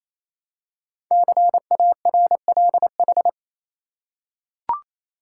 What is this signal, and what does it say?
Morse code, heard as a keyed tone, about 29 wpm, 700 Hz. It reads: CARLH
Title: Code Groups - 5 Letters - 28wpm